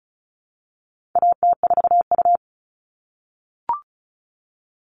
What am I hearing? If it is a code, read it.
Morse code, heard as a keyed tone, about 34 wpm, 700 Hz. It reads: AT4U
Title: All Cut Numbers - 4-Digit 35wpm